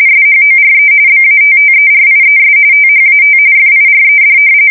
RTTY, about 45.45 baud. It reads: SHORE BATTERIES THAT FIRED
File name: fsk170 / battleshipnj